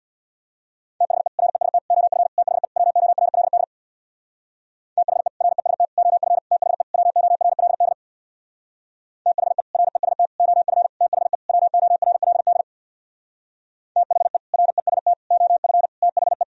Morse code, heard as a keyed tone, about 64 wpm, 700 Hz. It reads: THE REST OF THE WORLD THE REST OF THE WORLD THE REST OF THE WORLD THE REST OF THE